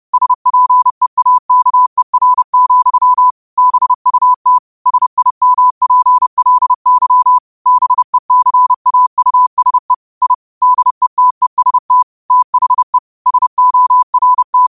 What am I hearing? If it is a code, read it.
Morse code, heard as a keyed tone, about 30 wpm, 1000 Hz. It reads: SPEAKER, BUT SIMPLY BECAUSE I DETEST THE SORT